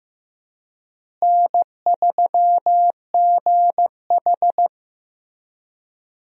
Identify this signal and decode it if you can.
Morse code, heard as a keyed tone, about 15 wpm, 700 Hz. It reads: N3GH
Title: USA Call Signs - 4 Characters - Repeated 3x - 15wpm